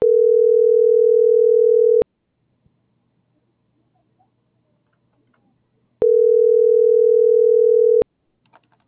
Ringback tone (North America), 440 + 480 Hz.